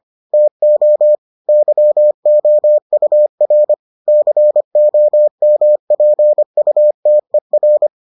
Morse code, heard as a keyed tone, about 25 wpm, 600 Hz. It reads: TO YOUR COMPUTER